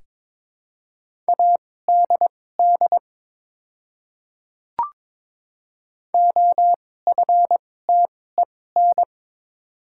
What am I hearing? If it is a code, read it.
Morse code, heard as a keyed tone, about 22 wpm, spaced as for 16 wpm, 700 Hz. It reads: ADD OFTEN
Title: 2x Character Spacing - Top 300 Words - 22wpm